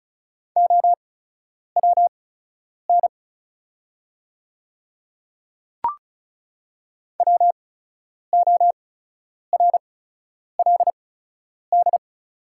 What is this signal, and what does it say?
Morse code, heard as a keyed tone, about 36 wpm, spaced as for 10 wpm, 700 Hz. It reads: OWN WORLD
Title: 8x Character Spacing - Top 300 Words - 35wpm